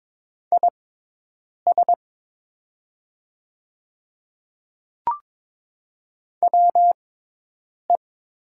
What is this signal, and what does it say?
Morse code, heard as a keyed tone, about 22 wpm, spaced as for 8 wpm, 700 Hz. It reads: IS WE
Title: 6x Character Spacing - Top 100 Words - 22wpm